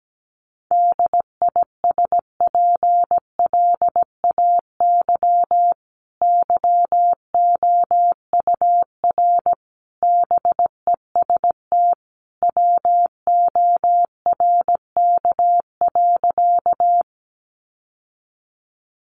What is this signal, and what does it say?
Morse code, heard as a keyed tone, about 17 wpm, 700 Hz. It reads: DISPLAY YOUR BEST WORK.
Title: Sentences from Top 1200 Words 17wpm